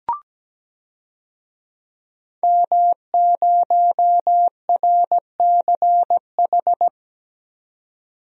Morse code, heard as a keyed tone, about 17 wpm, 700 Hz. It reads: M0RCH
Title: DXCC-100 Call Signs - 5 Characters - 17wpm